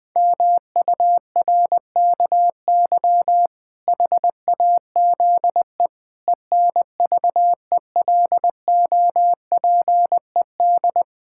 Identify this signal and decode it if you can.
Morse code, heard as a keyed tone, about 21 wpm, 700 Hz. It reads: MURKY HAZE ENVELOPED
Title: Pangram Challenge v2 - 20wpm